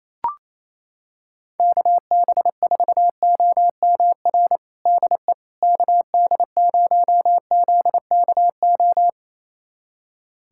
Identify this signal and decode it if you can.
Morse code, heard as a keyed tone, about 28 wpm, 700 Hz. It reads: KB4OMR DE KD0ZKO
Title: USA Callsign DE Callsign - 6 Characters - 28wpm